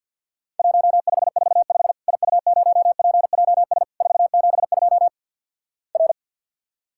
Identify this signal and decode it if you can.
Morse code, heard as a keyed tone, about 48 wpm, 700 Hz. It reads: 1545 IU0PJS 472 R
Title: Worked All Europe DX Contest - QTC Exchange with Repeats - 50wpm